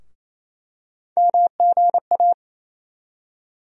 Morse code, heard as a keyed tone, about 28 wpm, 700 Hz. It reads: MGA
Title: USA Call Sign Postfixes - 3 Letters 28wpm